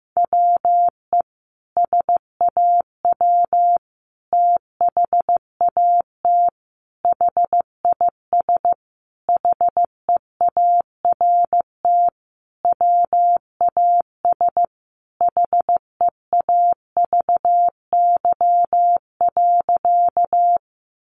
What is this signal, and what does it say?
Morse code, heard as a keyed tone, about 15 wpm, 700 Hz. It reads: WE SAW THAT HIS HEART WAS HEAVY.
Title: Sentences from Top 1200 Words 15wpm